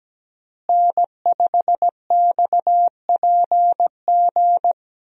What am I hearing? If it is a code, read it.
Morse code, heard as a keyed tone, about 17 wpm, 700 Hz. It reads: N5XPG
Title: USA Call Signs - 5 Characters - Repeated 3x - 17wpm